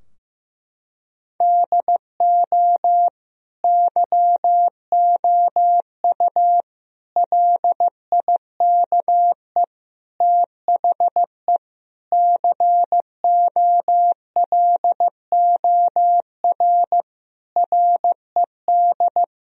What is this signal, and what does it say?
Morse code, heard as a keyed tone, about 15 wpm, 700 Hz. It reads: DO YOU LIKE THE COLOR RED
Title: Sentences from Top 1000 Words - Review Part 2 - 15wpm